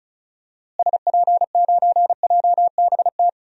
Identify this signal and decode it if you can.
Morse code, heard as a keyed tone, about 35 wpm, 700 Hz. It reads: SP9JBT
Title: DXCC-100 Call Signs - 6 Characters - Repeated 3x - 35wpm